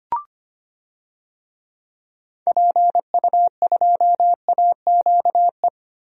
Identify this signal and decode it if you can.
Morse code, heard as a keyed tone, about 25 wpm, 700 Hz. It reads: PU2AQE
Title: DXCC-100 Call Signs - 6 Characters - Repeated 2x - 25wpm